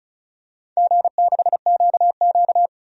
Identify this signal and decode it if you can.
Morse code, heard as a keyed tone, about 34 wpm, 700 Hz. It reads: G6QQ
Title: DXCC-100 Call Signs - 4 Characters - 35wpm